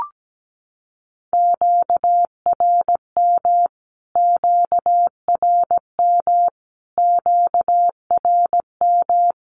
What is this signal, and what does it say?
Morse code, heard as a keyed tone, about 17 wpm, 700 Hz. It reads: QRM QRM QRM
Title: Q-Codes Repeated 3x 17wpm